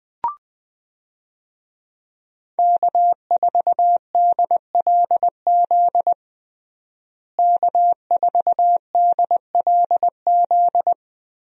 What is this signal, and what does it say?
Morse code, heard as a keyed tone, about 20 wpm, 700 Hz. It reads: K4DLZ K4DLZ
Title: USA Call Signs - 5 Characters - Repeated 3x - 20wpm